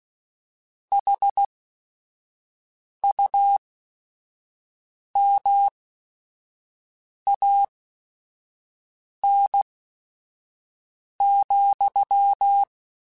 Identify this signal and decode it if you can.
Morse code, heard as a keyed tone, about 16 wpm, spaced as for 5 wpm, 800 Hz. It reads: HUMAN,
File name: Chuckles / Chuckles-mp3s / HumanErr @5